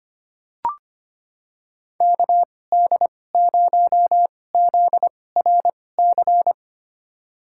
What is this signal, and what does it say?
Morse code, heard as a keyed tone, about 25 wpm, spaced as for 18 wpm, 700 Hz. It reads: KD0ZRC
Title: USA Call Signs - 6 Characters - 2x Character Spacing - 25wpm